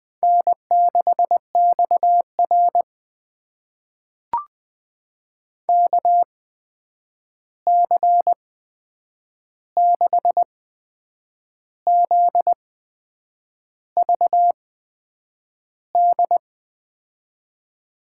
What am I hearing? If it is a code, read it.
Morse code, heard as a keyed tone, about 20 wpm, spaced as for 8 wpm, 700 Hz. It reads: N6XR KC6ZVD
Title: USA Call Signs - 8x Character Spacing - 20wpm